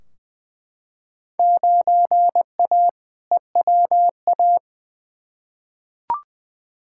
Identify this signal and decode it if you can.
Morse code, heard as a keyed tone, about 20 wpm, 700 Hz. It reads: 9A EWA
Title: Field Day Exchanges Encore - 20wpm